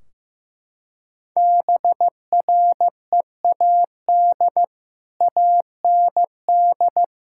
Morse code, heard as a keyed tone, about 15 wpm, 700 Hz. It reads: BREAD AND
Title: Sentences from Top 1600 Words 15wpm